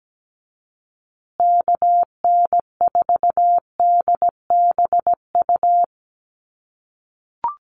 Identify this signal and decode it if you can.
Morse code, heard as a keyed tone, about 17 wpm, 700 Hz. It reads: KN4DBU
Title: USA Call Signs Repeated 3x 17wpm